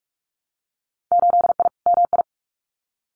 Morse code, heard as a keyed tone, about 46 wpm, 700 Hz. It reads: 8I MI
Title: Winter Field Day Exchanges Encore 45wpm